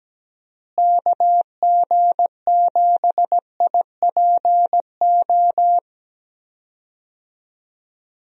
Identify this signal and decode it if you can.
Morse code, heard as a keyed tone, about 17 wpm, 700 Hz. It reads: KG7IPO
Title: USA Call Signs - 6 Characters - Repeated 2x - 17wpm